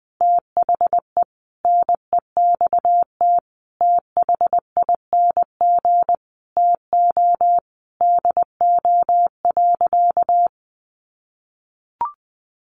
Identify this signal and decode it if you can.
Morse code, heard as a keyed tone, about 20 wpm, 700 Hz. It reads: THE NEXT THING TO DO.
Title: Sentences from Top 1300 Words 20wpm